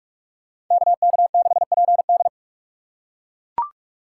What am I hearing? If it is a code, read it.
Morse code, heard as a keyed tone, about 42 wpm, 700 Hz. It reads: KK6PD
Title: USA Call Signs - 4x Character Spacing - 45wpm